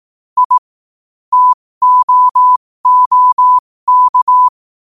Morse code, heard as a keyed tone, about 18 wpm, spaced as for 15 wpm, 1000 Hz. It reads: I TOOK
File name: qotdfiles / 15wpm